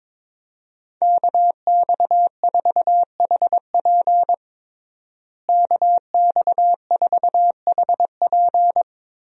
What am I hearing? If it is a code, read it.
Morse code, heard as a keyed tone, about 22 wpm, 700 Hz. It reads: KX4HP KX4HP
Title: USA Call Signs - 5 Characters - Repeated 3x - 22wpm